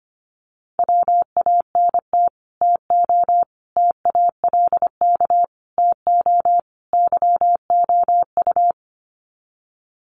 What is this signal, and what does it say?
Morse code, heard as a keyed tone, about 25 wpm, 700 Hz. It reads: WANT TO TALK TO YOU
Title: Sets of 5 Words Encore 25wpm